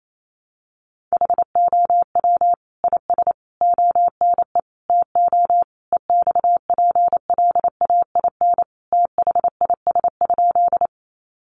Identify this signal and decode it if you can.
Morse code, heard as a keyed tone, about 28 wpm, 700 Hz. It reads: HOW IS ONE TO EXPLAIN THIS?
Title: Sentences from Top 1100 Words 28wpm